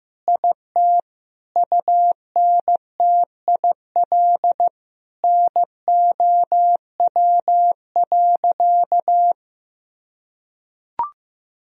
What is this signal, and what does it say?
Morse code, heard as a keyed tone, about 16 wpm, 700 Hz. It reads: IT UNTIL NOW.
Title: Sentences from Top 1000 Words - Review Part 1 - 15wpm